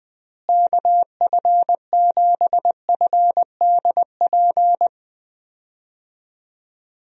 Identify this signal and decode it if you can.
Morse code, heard as a keyed tone, about 20 wpm, 700 Hz. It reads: KF7FDP
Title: USA Call Signs - 6 Characters - Repeated 3x - 20wpm